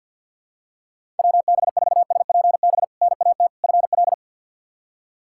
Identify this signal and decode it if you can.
Morse code, heard as a keyed tone, about 47 wpm, 700 Hz. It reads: WB4SPB NAT FL
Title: K1USN SST QSO Practice - 50wpm